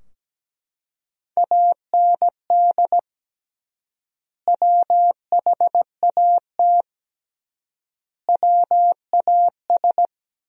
Morse code, heard as a keyed tone, about 17 wpm, 700 Hz. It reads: AND WHAT WAS
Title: 3x Word-Spacing - Sentences from Top 100 Words - 17wpm